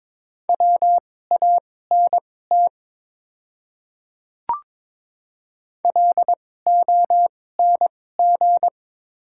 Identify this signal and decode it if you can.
Morse code, heard as a keyed tone, about 22 wpm, spaced as for 16 wpm, 700 Hz. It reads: WANT LONG
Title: 2x Character Spacing - Top 200 Words - 22wpm